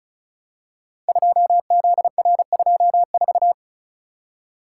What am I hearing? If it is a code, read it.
Morse code, heard as a keyed tone, about 36 wpm, 700 Hz. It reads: 2ZR24
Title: Code Groups - 5 Letter:Numbers - 35wpm